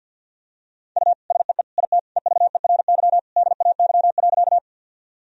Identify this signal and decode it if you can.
Morse code, heard as a keyed tone, about 53 wpm, 700 Hz. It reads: U SEE IT EVERY DAY.
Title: Sentences from Top 200 Words and Abbreviations 50wpm